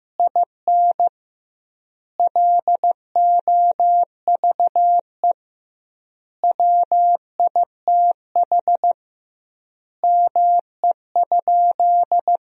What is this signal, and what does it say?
Morse code, heard as a keyed tone, about 15 wpm, 700 Hz. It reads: IN LOVE WITH ME?
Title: 2x Word-Spacing - Sentences from Top 400 Words - 15wpm